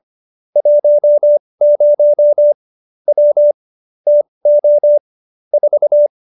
Morse code, heard as a keyed tone, about 25 wpm, spaced as for 20 wpm, 590 Hz. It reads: 10 W TO 4